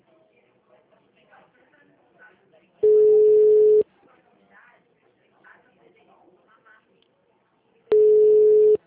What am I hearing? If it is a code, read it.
Ringback tone (Europe, 425 Hz), with speech.